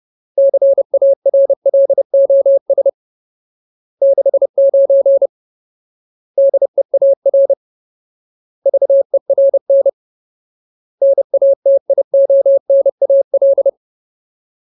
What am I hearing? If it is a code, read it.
Morse code, heard as a keyed tone, about 30 wpm, 550 Hz. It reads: CARLOS 69 DEAR VERN NATIONAL
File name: Ultimate Word List / 30WPM Ultimate